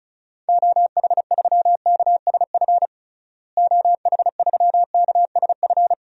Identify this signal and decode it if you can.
Morse code, heard as a keyed tone, about 36 wpm, 700 Hz. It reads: OH3KSF OH3KSF
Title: DXCC-100 Call Signs - 6 Characters - Repeated 2x - 35wpm